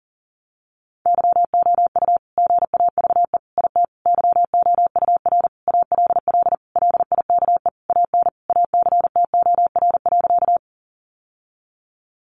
Morse code, heard as a keyed tone, about 40 wpm, 700 Hz. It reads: YOU GAVE IT YOUR ALL LIKE AN ACTOR.
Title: Sentences from Top 1500 Words 40wpm